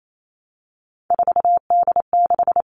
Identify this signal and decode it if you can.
Morse code, heard as a keyed tone, about 28 wpm, 700 Hz. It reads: 4D6
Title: All Cut Numbers - 3-Digit 28wpm